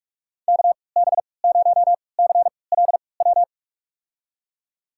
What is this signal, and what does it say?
Morse code, heard as a keyed tone, about 48 wpm, spaced as for 26 wpm, 700 Hz. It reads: KB0CLW
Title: USA Call Signs - 3x Character Spacing - 45wpm